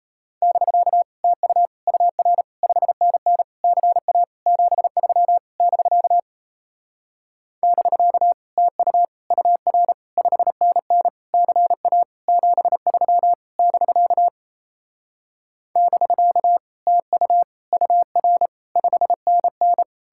Morse code, heard as a keyed tone, about 38 wpm, 700 Hz. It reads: <BK> TU UR 5NN CA 73 <BK> <BK> TU UR 5NN CA 73 <BK> <BK> TU UR 5NN